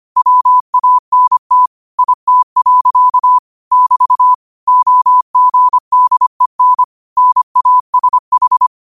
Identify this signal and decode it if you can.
Morse code, heard as a keyed tone, about 25 wpm, 1000 Hz. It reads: WANT IT. <BT> OGDEN NASH
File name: qotdfiles / 25wpm